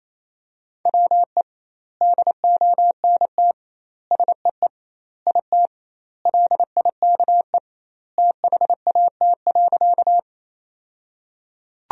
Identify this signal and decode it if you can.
Morse code, heard as a keyed tone, about 27 wpm, 700 Hz. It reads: WE DONT SEE IT LIKE THAT.
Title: 2x Word-Spacing - Sentences from Top 300 Words - 28wpm